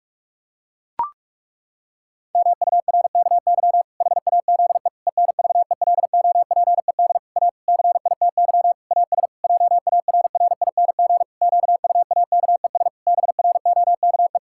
Morse code, heard as a keyed tone, about 41 wpm, 700 Hz. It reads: MURKY HAZE ENVELOPED A CITY AS JARRING QUAKES BROKE
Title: Pangram Challenge v2 - 45wpm